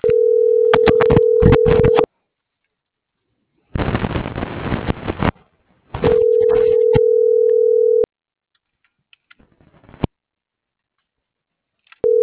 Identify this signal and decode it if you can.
Ringback tone (North America), 440 + 480 Hz.